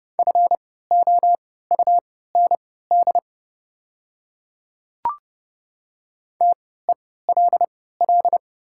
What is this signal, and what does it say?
Morse code, heard as a keyed tone, about 30 wpm, spaced as for 17 wpm, 690 Hz. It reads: FOUND TELL
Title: 3x Character Spacing - Top 300 Words - 30wpm